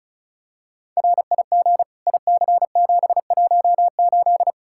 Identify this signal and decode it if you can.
Morse code, heard as a keyed tone, about 34 wpm, 700 Hz. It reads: RIG IC718
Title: Practice QSO - Exchange 2 - Info - 35wpm